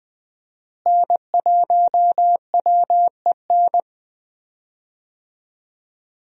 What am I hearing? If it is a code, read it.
Morse code, heard as a keyed tone, about 20 wpm, 700 Hz. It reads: N1WEN
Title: USA Call Signs - 5 Characters - Repeated 2x - 20wpm